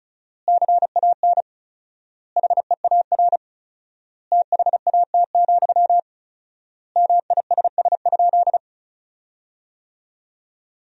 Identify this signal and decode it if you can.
Morse code, heard as a keyed tone, about 36 wpm, 700 Hz. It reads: CAN HEAR THAT, MISS?
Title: 4x Word-Spacing - Sentences from Top 500 Words - 35wpm